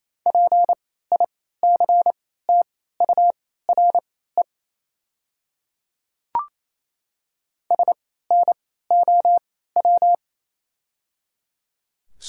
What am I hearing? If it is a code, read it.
Morse code, heard as a keyed tone, about 28 wpm, spaced as for 16 wpm, 700 Hz. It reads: PICTURE SNOW
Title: 3x Character Spacing - Top 500 Words - 28wpm